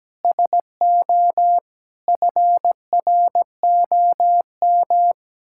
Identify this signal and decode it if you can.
Morse code, heard as a keyed tone, about 17 wpm, 700 Hz. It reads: SO FROM
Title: Sentences from Top 1600 Words 17wpm